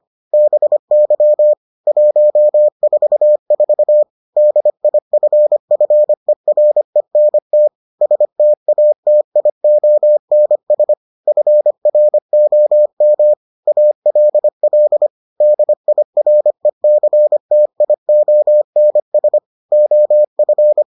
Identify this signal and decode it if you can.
Morse code, heard as a keyed tone, about 25 wpm, 600 Hz. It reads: BY 144 DIFFERENT STATIONS FROM ALL DIRECTIONS OF